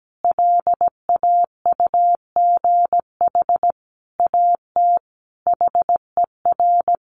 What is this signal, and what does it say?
Morse code, heard as a keyed tone, about 17 wpm, 700 Hz. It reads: LAUGH AT HER
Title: Sentences from Top 600 Words 17wpm